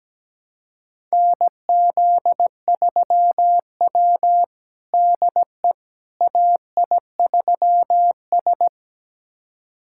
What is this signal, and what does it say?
Morse code, heard as a keyed tone, about 17 wpm, 700 Hz. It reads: NZ3W DE AI3S
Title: USA Callsign DE Callsign - 4 Characters - 17wpm